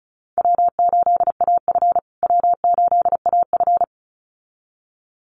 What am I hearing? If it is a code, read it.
Morse code, heard as a keyed tone, about 34 wpm, 700 Hz. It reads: W8AF W8AF
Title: USA Call Signs Repeated 2x 35wpm